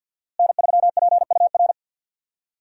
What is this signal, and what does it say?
Morse code, heard as a keyed tone, about 50 wpm, 700 Hz. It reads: N3PUR
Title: USA Call Signs - 5 Characters - Repeated 3x - 50wpm